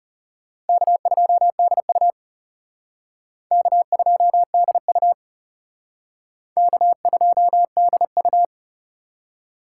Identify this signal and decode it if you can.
Morse code, heard as a keyed tone, about 36 wpm, 700 Hz. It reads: K2DU K2DU K2DU
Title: Speed Racing - USA 1x2 Call Signs 30wpm